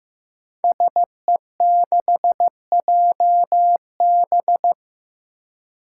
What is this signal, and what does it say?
Morse code, heard as a keyed tone, about 15 wpm, 700 Hz. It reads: SE6JB
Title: Five Challenging Letter-Number Combinations 15wpm